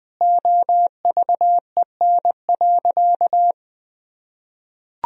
Morse code, heard as a keyed tone, about 21 wpm, 700 Hz. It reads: OVEN.
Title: Sentences from Top 1000 Words - Review Part 4 - 20wpm